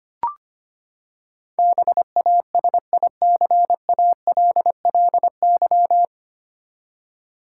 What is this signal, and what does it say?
Morse code, heard as a keyed tone, about 25 wpm, 700 Hz. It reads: BASICALLY
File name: Nine Letter Words - 25wpm